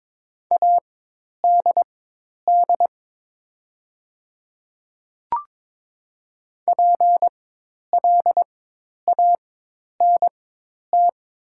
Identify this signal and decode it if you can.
Morse code, heard as a keyed tone, about 22 wpm, spaced as for 10 wpm, 700 Hz. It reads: ADD PLANT
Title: 4x Character Spacing - Top 300 Words - 22wpm